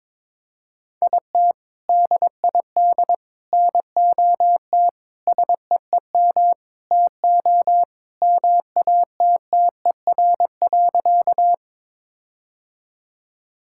Morse code, heard as a keyed tone, about 22 wpm, 700 Hz. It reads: IT DID NOT SEEM TO MATTER.
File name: practice / 22wpm / Sentences with 4-Word Phrases - 22wpm